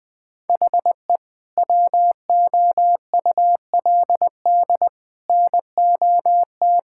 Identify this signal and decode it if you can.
Morse code, heard as a keyed tone, about 20 wpm, 700 Hz. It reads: HE WOULD NOT
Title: Sentences from Top 1000 Words - Review Part 5 - 20wpm